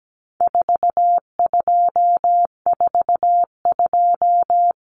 Morse code, heard as a keyed tone, about 17 wpm, 700 Hz. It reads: 4242
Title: Numbers - 4-Digit 17wpm